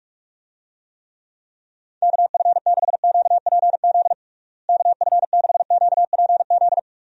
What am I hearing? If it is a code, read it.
Morse code, heard as a keyed tone, about 46 wpm, 700 Hz. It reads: KF6QPZ KF6QPZ
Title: USA Call Signs - 6 Characters - Repeated 3x - 45wpm